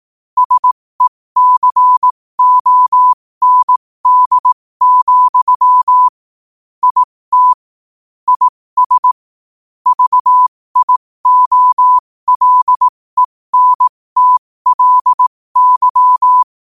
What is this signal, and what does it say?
Morse code, heard as a keyed tone, about 18 wpm, spaced as for 15 wpm, 1000 Hz. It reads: SECOND, IT IS VIOLENTLY
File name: qotdfiles / 15wpm